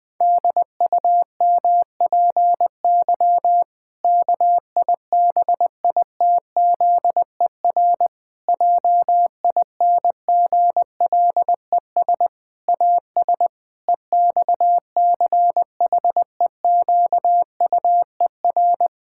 Morse code, heard as a keyed tone, about 20 wpm, 700 Hz. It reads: DUMPY KIBITZER JINGLES AS EXCHEQUER
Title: Pangram Challenge v2 - 20wpm